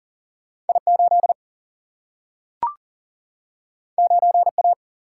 Morse code, heard as a keyed tone, about 40 wpm, 700 Hz. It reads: I 8 9 A
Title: Single-Letter Single-Number 40wpm